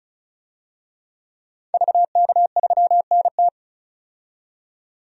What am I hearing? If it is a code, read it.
Morse code, heard as a keyed tone, about 34 wpm, 700 Hz. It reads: VK3NT
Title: DXCC-100 Call Signs - 5 Characters - 35wpm